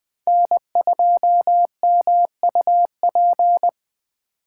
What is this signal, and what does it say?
Morse code, heard as a keyed tone, about 23 wpm, 700 Hz. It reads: N2MUP
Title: Speed Racing - USA 1x3 Call Signs 20wpm